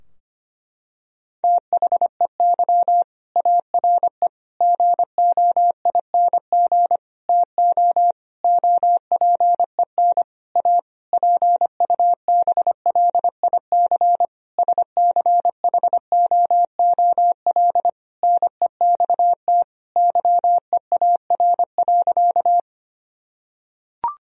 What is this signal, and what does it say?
Morse code, heard as a keyed tone, about 25 wpm, 700 Hz. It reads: THEY ARE GOING TO OPEN A PUBLIC SCHOOL NEXT YEAR.
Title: Sentences from Top 1000 Words - Review Part 3 - 25wpm